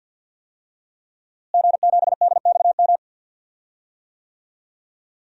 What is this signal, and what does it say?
Morse code, heard as a keyed tone, about 52 wpm, 700 Hz. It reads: G7DXM
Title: DXCC-100 Call Signs - 5 Characters - Repeated 2x - 50wpm